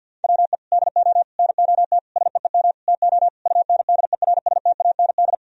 Morse code, heard as a keyed tone, about 47 wpm, 700 Hz. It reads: WE DO NOT SEEM TO UNDERSTAND
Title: Sentences from Top 1000 Words - Review Part 1 - 50wpm